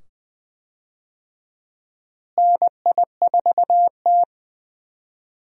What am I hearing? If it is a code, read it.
Morse code, heard as a keyed tone, about 21 wpm, 700 Hz. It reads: NI4T
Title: USA Call Signs - 4 Characters - Repeated 2x - 20wpm